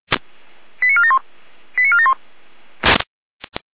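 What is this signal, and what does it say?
Five-tone selective call (ZVEI), heard as quick runs of five tones, tones of about 70 ms. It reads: 89471 89471
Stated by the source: ZVEI five-tone selective call received with an SDR.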